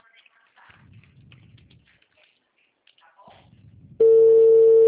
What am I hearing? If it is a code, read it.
Ringback tone (China, 450 Hz).